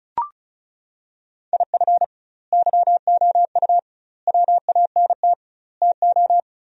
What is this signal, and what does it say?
Morse code, heard as a keyed tone, about 36 wpm, 700 Hz. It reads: IF YOU WANT TO
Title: 2x Word-Spacing - Sentences from Top 200 Words - 35wpm